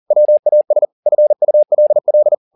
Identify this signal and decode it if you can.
Morse code, heard as a keyed tone, about 40 wpm, 600 Hz. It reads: WAS FULL
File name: the old man and the sea 2 40wpm